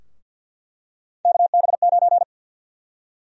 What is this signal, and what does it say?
Morse code, heard as a keyed tone, about 52 wpm, 700 Hz. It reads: KB9
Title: USA Call Sign Prefixes - 50wpm